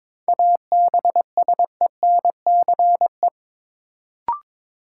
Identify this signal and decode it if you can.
Morse code, heard as a keyed tone, about 22 wpm, 700 Hz. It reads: ABSENCE
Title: Sets of 4 Words - New Word Drill - 22wpm